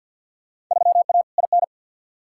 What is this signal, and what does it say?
Morse code, heard as a keyed tone, about 52 wpm, 700 Hz. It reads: 3A IN
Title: Field Day Exchanges Encore - 50wpm